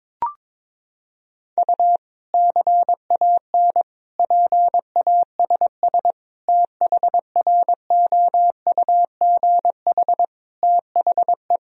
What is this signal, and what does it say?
Morse code, heard as a keyed tone, about 22 wpm, 700 Hz. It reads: U CAN PASS THROUGH THE
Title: Sentences from Top 400 Words and Abbreviations 22wpm